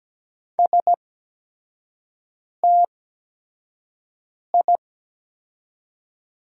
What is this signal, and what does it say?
Morse code, heard as a keyed tone, about 17 wpm, spaced as for 5 wpm, 700 Hz. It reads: STI
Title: 8x Character Spacing - Top 300 Words - 17wpm